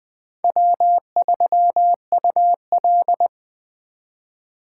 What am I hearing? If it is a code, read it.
Morse code, heard as a keyed tone, about 23 wpm, 700 Hz. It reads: W3UL
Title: Speed Racing - USA Call Signs 20wpm